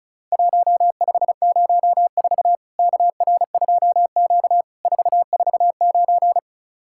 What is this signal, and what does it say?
Morse code, heard as a keyed tone, about 36 wpm, 700 Hz. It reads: 1504 KR2Q 449
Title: Worked All Europe DX Contest - QTC Exchange - 35wpm